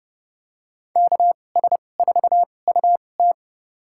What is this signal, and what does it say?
Morse code, heard as a keyed tone, about 30 wpm, spaced as for 22 wpm, 700 Hz. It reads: KS4UT
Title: USA Call Signs - 5 Characters - 2x Character Spacing - 30wpm